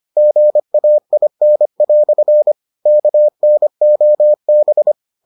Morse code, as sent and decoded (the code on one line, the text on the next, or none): --. .- .. -. .-..-. / -.- -. --- -...
GAIN" KNOB